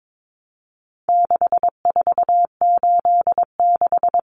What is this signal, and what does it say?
Morse code, heard as a keyed tone, about 22 wpm, 700 Hz. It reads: 6486
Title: Numbers - 4-Digit 22wpm